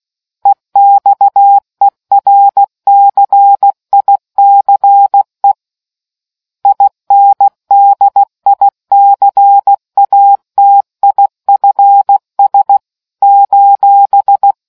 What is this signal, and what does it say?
Morse code, heard as a keyed tone, about 16 wpm, 800 Hz. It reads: EXERCICE INDICATIFS :